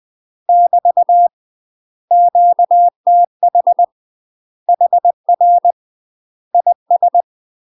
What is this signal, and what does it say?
Morse code, heard as a keyed tone, about 20 wpm, 700 Hz. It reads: <BT> QTH HR IS
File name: UK MASTER QSOs TRAINING / 20WPM / UK MASTER QSOs 0051